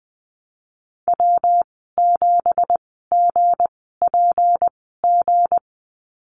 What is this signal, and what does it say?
Morse code, heard as a keyed tone, about 20 wpm, spaced as for 15 wpm, 700 Hz. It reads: W7GPG
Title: 173 Lesson 28g - Number 9 - Call Signs 20wpm